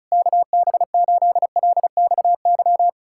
Morse code, heard as a keyed tone, about 38 wpm, 700 Hz. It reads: KB8LXY